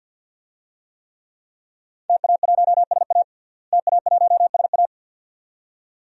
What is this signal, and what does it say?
Morse code, heard as a keyed tone, about 52 wpm, 700 Hz. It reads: TA1SA TA1SA
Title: DXCC-100 Call Signs - 5 Characters - Repeated 2x - 50wpm